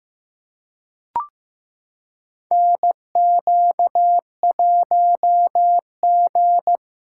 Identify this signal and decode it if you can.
Morse code, heard as a keyed tone, about 15 wpm, 700 Hz. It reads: NQ1G
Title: USA Callsign DE Callsign - 4 Characters - 15wpm